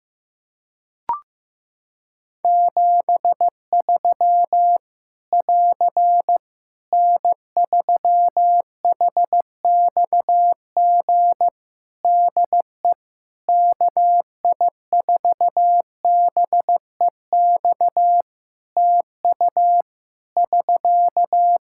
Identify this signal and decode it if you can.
Morse code, heard as a keyed tone, about 14 wpm, 700 Hz. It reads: 73 <AR> N3HXG DE KI4BEX TU <SK>
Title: K1USN SST QSO Practice 15wpm